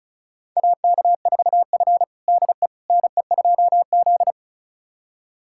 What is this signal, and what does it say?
Morse code, heard as a keyed tone, about 36 wpm, 700 Hz. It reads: AK4F DE NE2Z
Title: USA Callsign DE Callsign - 4 Characters - 35wpm